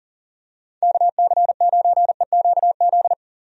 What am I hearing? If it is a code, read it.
Morse code, heard as a keyed tone, about 40 wpm, 700 Hz. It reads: KC9EQZ
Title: USA Call Signs - 6 Characters - 40wpm